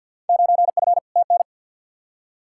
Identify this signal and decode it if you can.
Morse code, heard as a keyed tone, about 47 wpm, 700 Hz. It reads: 9F TN
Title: Field Day Exchanges Encore - 50wpm